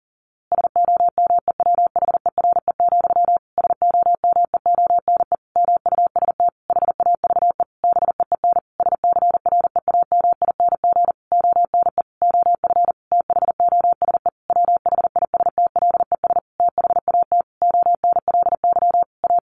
Morse code, heard as a keyed tone, about 40 wpm, 700 Hz. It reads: SOMEWHERE, SOMEONE MUST HAVE BEEN SCREAMING ONE OF THOSE WHISTLES THAT ONLY A
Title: A Sound Of Thunder 40wpm